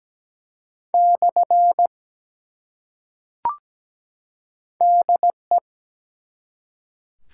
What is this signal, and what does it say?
Morse code, heard as a keyed tone, about 17 wpm, 700 Hz. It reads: / DE
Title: Top 180 QSO Elements 17wpm